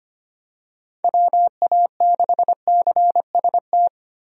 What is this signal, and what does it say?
Morse code, heard as a keyed tone, about 25 wpm, 700 Hz. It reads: WA6CST
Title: USA Call Signs - 6 Characters - Repeated 3x - 25wpm